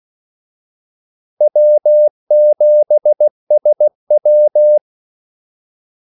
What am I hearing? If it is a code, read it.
Morse code, heard as a keyed tone, about 16 wpm, 600 Hz. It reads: W7SW
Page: Kaldesignaler 16 wpm